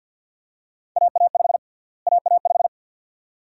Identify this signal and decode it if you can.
Morse code, heard as a keyed tone, about 51 wpm, 700 Hz. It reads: AA5 AA5
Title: USA Call Sign Prefixes - Repeated 2x - 50wpm